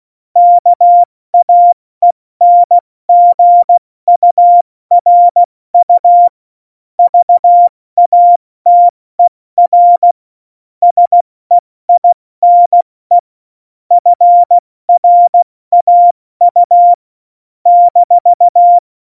Morse code, heard as a keyed tone, about 16 wpm, 700 Hz. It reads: KAENGURU VATER SEINE FRAU -
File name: Morsekurs / Witze / Witz 16. Oktober 2018, 70 F80